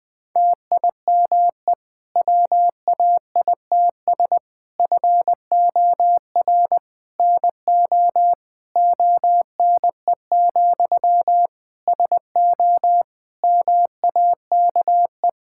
Morse code, heard as a keyed tone, about 20 wpm, 700 Hz. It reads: TIME WAITS FOR NO ONE, SO MAKE
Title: Difficult Sentences with Hints - 20wpm